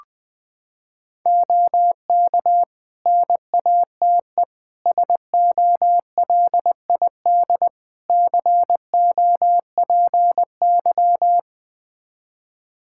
Morse code, heard as a keyed tone, about 20 wpm, 700 Hz. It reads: OK NATE SOLID COPY
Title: Practice QSO - Exchange 3 - More Info - 20wpm